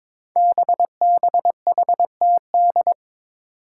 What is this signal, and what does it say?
Morse code, heard as a keyed tone, about 22 wpm, 700 Hz. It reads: BBHTD
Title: Five Challenging Letter-Number Combinations 22wpm